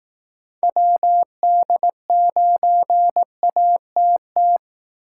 Morse code, heard as a keyed tone, about 18 wpm, 700 Hz. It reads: WD9ATT
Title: Speed Racing - USA 2x3 Call Signs 15wpm